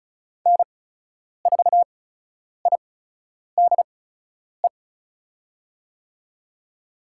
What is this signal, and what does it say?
Morse code, heard as a keyed tone, about 34 wpm, spaced as for 10 wpm, 700 Hz. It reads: N4IDE
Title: USA Call Signs - 8x Character Spacing - 35wpm